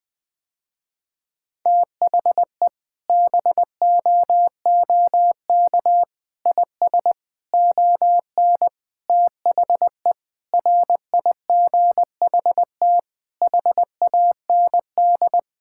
Morse code, heard as a keyed tone, about 20 wpm, 700 Hz. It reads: THE BOOK IS ON THE RIGHT HAND
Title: Sentences with 7-Word Phrases - Spoken Beginning - 20wpm